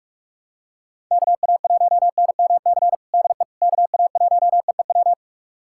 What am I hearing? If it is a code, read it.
Morse code, heard as a keyed tone, about 42 wpm, 700 Hz. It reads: KA1NMC DE KA1EEW
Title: USA Callsign DE Callsign 45wpm